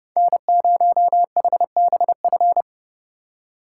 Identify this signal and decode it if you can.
Morse code, heard as a keyed tone, about 36 wpm, 700 Hz. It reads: N0HBF
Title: Speed Racing - USA 1x3 Call Signs 30wpm